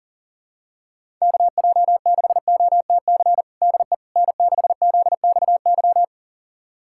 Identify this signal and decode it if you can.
Morse code, heard as a keyed tone, about 40 wpm, 700 Hz. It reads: KJ6OTC DE N6ZXY
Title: USA Callsign DE Callsign 40wpm